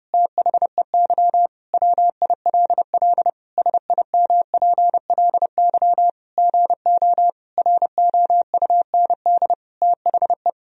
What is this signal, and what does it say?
Morse code, heard as a keyed tone, about 30 wpm, 700 Hz. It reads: THEY WILL SIMPLY GO ROUND THE
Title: Sentences from Top 1000 Words - Review Part 4 - 30wpm